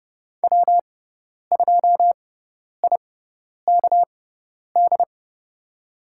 Morse code, heard as a keyed tone, about 30 wpm, spaced as for 10 wpm, 700 Hz. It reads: W2IKD
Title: USA Call Signs - 6x Character Spacing - 30wpm